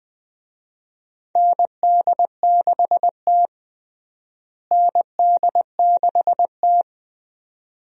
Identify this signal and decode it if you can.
Morse code, heard as a keyed tone, about 20 wpm, 700 Hz. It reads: ND6T ND6T
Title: USA Call Signs - 4 Characters - Repeated 2x - 20wpm